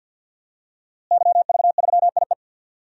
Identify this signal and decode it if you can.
Morse code, heard as a keyed tone, about 53 wpm, 700 Hz. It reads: YV3IE
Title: DXCC-100 Call Signs - 5 Characters - Repeated 2x - 50wpm